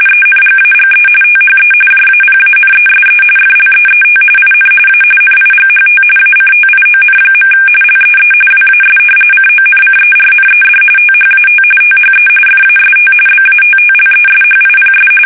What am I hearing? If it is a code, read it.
RTTY, about 45.45 baud. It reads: ENGAGE THE JAPANESE FLEET IN A NIGHT ACTION. LEE WAS CONCERNED THAT THE FAST BATTLESHIPS
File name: fsk850 / battleshipnj